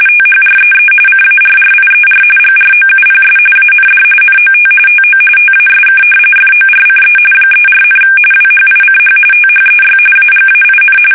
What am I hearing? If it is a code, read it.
RTTY, about 45.45 baud. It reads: THREE CARRIER GROUPS AND NOT TO TASK FORCE 34, WHICH THEY AGAIN
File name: fsk850 / battleshipnj